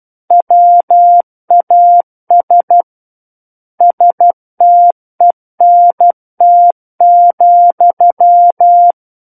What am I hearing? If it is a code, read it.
Morse code, heard as a keyed tone, about 12 wpm, 700 Hz. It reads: WAS STENT,
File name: Book One / 03-Chapter 3 - 12 WPM